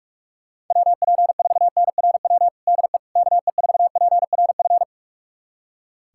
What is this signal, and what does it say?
Morse code, heard as a keyed tone, about 46 wpm, 700 Hz. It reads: WP4NRW DE KE4PRF
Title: USA Callsign DE Callsign - 6 Characters - 45wpm